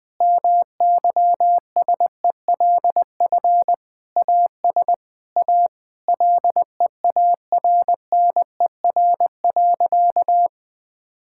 Morse code, heard as a keyed tone, about 20 wpm, 700 Hz. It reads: MYSELF AS A LEARNER.
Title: Sentences with 8-Word Phrases - 20wpm